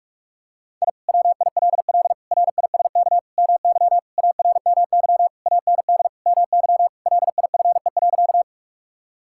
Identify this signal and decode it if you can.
Morse code, heard as a keyed tone, about 46 wpm, 700 Hz. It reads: I WILL RISK MY ARMY AND MY LIFE.
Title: Sentences from Top 1000 Words - Review Part 5 - 45wpm